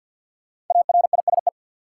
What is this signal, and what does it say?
Morse code, heard as a keyed tone, about 52 wpm, 700 Hz. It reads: ARISE
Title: Five Letter Words - 50wpm